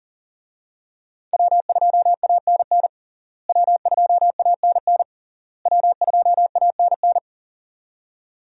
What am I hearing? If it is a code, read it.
Morse code, heard as a keyed tone, about 40 wpm, 700 Hz. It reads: W2ANN W2ANN W2ANN
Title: USA Call Signs - 5 Characters - Repeated 3x - 40wpm